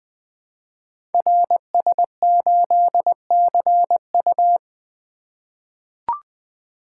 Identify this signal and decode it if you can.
Morse code, heard as a keyed tone, about 20 wpm, 700 Hz. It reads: RS8CU
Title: Code Groups - 5 Letter:Numbers - 20wpm